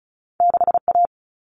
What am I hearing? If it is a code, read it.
Morse code, heard as a keyed tone, about 36 wpm, 690 Hz. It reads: 6A
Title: All Cut Numbers - 2-Digit 35wpm